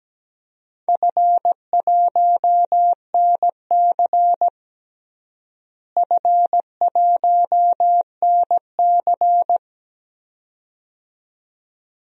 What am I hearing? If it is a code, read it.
Morse code, heard as a keyed tone, about 17 wpm, 700 Hz. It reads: F1NC F1NC
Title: DXCC-100 Call Signs - 4 Characters - Repeated 2x - 17wpm